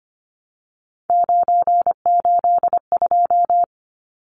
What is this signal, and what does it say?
Morse code, heard as a keyed tone, about 25 wpm, 700 Hz. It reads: 982
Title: Numbers - 3-Digit 25wpm